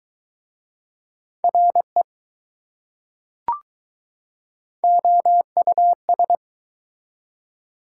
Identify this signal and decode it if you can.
Morse code, heard as a keyed tone, about 21 wpm, 700 Hz. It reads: RE OUS